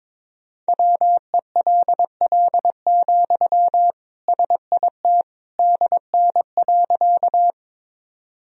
Morse code, heard as a keyed tone, about 22 wpm, 700 Hz. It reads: WELL, SIT DN.
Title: Sentences from Top 500 Words and Abbreviations 22wpm